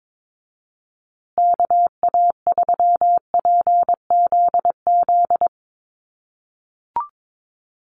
Morse code, heard as a keyed tone, about 22 wpm, 700 Hz. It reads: KA3PZZ
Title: USA Call Signs Repeated 2x 22wpm